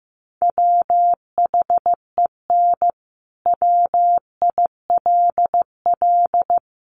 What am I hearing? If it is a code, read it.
Morse code, heard as a keyed tone, about 15 wpm, 700 Hz. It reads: WHEN WILL
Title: Sentences from Top 1400 Words 15wpm